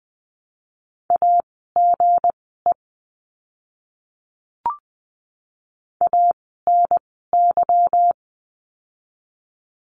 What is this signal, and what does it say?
Morse code, heard as a keyed tone, about 20 wpm, spaced as for 15 wpm, 700 Hz. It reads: AGE ANY
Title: 160 Lesson 26f - Letter g - Words 20wpm